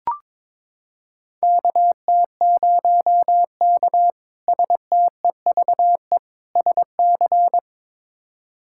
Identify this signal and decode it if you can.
Morse code, heard as a keyed tone, about 22 wpm, 700 Hz. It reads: KT0K STEVE SC
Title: K1USN SST QSO Practice - 22wpm